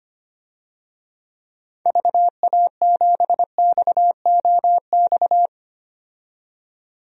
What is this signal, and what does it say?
Morse code, heard as a keyed tone, about 25 wpm, 700 Hz. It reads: VA7XOX
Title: DXCC-100 Call Signs - 6 Characters - 25wpm